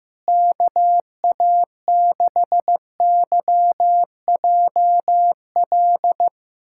Morse code, heard as a keyed tone, about 15 wpm, 700 Hz. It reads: KA6YJL
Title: USA Callsign DE Callsign - 6 Characters - 15wpm